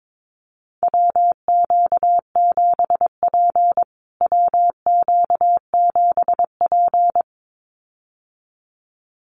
Morse code, heard as a keyed tone, about 22 wpm, 700 Hz. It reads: WQ7P WQ7P
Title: USA Call Signs Repeated 2x 22wpm